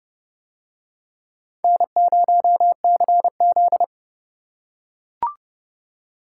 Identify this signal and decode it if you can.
Morse code, heard as a keyed tone, about 29 wpm, 700 Hz. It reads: N0CZ